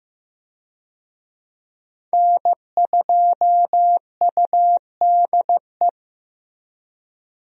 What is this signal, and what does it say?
Morse code, heard as a keyed tone, about 15 wpm, 700 Hz. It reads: N2UDE
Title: USA Call Signs - 5 Characters - Repeated 3x - 15wpm